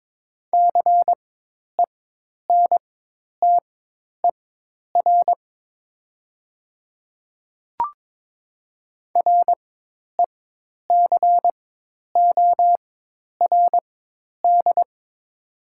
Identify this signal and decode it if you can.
Morse code, heard as a keyed tone, about 22 wpm, spaced as for 10 wpm, 700 Hz. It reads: CENTER RECORD
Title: 4x Character Spacing - Top 500 Words - 22wpm